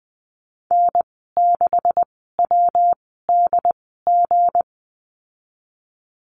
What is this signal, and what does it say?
Morse code, heard as a keyed tone, about 19 wpm, spaced as for 14 wpm, 700 Hz. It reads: N6WDG
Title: 238 Lesson 38g - Letter q - Call Signs 20wpm